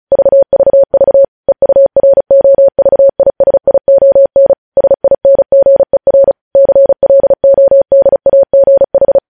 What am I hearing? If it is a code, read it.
Morse code, heard as a keyed tone, about 36 wpm, 550 Hz. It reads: VVV EUROVISION SINGER CLODAGH
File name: podcast / News headlines-1x-35x35